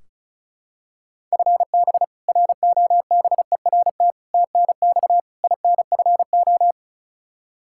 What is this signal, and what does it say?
Morse code, heard as a keyed tone, about 34 wpm, 700 Hz. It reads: FB ROBERT TNX INFO
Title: K1USN SST QSO Practice 35wpm